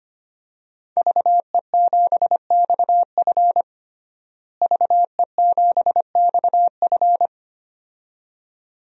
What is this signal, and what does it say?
Morse code, heard as a keyed tone, about 25 wpm, 700 Hz. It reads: VE7XF VE7XF
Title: DXCC-100 Call Signs - 5 Characters - Repeated 2x - 25wpm